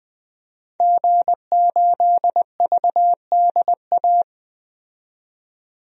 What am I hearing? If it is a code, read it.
Morse code, heard as a keyed tone, about 20 wpm, 700 Hz. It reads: G8VDA
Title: DXCC-100 Call Signs - 5 Characters - 20wpm